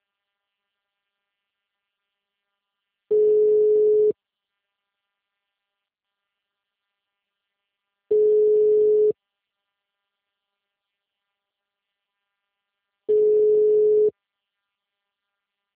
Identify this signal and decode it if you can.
Ringback tone (Europe, 425 Hz).
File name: out- -2028- - - .15